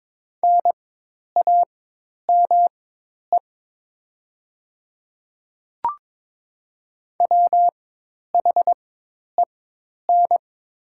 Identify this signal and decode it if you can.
Morse code, heard as a keyed tone, about 21 wpm, spaced as for 10 wpm, 700 Hz. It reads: NAME WHEN
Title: 4x Character Spacing - Top 300 Words - 22wpm